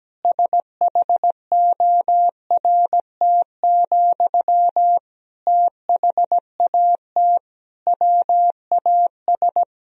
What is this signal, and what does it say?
Morse code, heard as a keyed tone, about 17 wpm, 700 Hz. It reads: SHORT, THAT WAS
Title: Sentences from Top 1000 Words - Review Part 2 - 17wpm